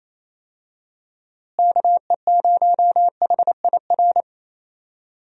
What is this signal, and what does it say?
Morse code, heard as a keyed tone, about 28 wpm, 700 Hz. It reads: KE0HIR
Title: USA Call Signs - 6 Characters - Repeated 3x - 28wpm